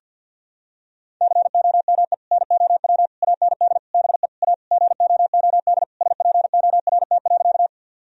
Morse code, heard as a keyed tone, about 50 wpm, 700 Hz. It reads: COME NOW AND BE A GOOD SPORT.
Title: Sentences from Top 1000 Words - Review Part 4 - 50wpm